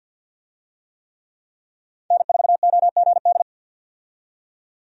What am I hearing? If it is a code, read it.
Morse code, heard as a keyed tone, about 47 wpm, 700 Hz. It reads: N4OGD
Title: USA Call Signs - 5 Characters - 50wpm